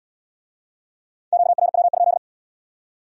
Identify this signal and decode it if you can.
Morse code, heard as a keyed tone, about 71 wpm, 700 Hz. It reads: QRW?